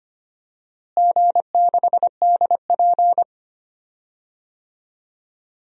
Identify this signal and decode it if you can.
Morse code, heard as a keyed tone, about 25 wpm, 700 Hz. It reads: G6DP
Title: DXCC-100 Call Signs - 4 Characters - Repeated 3x - 25wpm